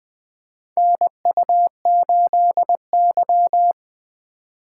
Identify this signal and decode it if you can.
Morse code, heard as a keyed tone, about 21 wpm, 700 Hz. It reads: NU8Y
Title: USA Call Signs - 4 Characters - 3x Character Spacing - 20wpm